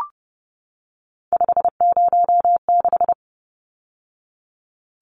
Morse code, heard as a keyed tone, about 30 wpm, 700 Hz. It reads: 506
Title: Numbers - 3-Digit 30wpm